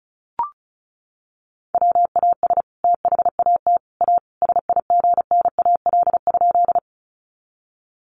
Morse code, heard as a keyed tone, about 34 wpm, 700 Hz. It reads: WAS THAT A SIGNAL?
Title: Sentences from Top 1400 Words 35wpm